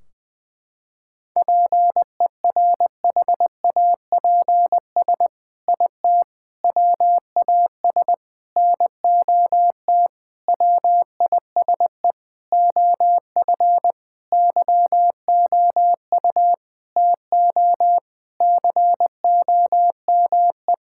Morse code, heard as a keyed tone, about 20 wpm, 700 Hz. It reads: PERHAPS IT WAS NOT WISE OF YOU TO COME
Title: Sentences from Top 1600 Words 20wpm